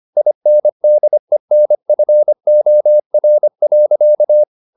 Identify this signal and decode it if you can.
Morse code, heard as a keyed tone, about 25 wpm, 600 Hz. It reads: INDENFOR.
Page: Ferie 24 wpm